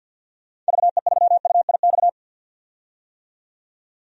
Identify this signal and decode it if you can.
Morse code, heard as a keyed tone, about 53 wpm, 700 Hz. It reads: VE3UIX
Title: DXCC-100 Call Signs - 6 Characters - 50wpm